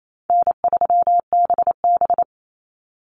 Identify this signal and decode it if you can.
Morse code, heard as a keyed tone, about 28 wpm, 700 Hz. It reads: N3BB
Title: Easy Call Signs - 28wpm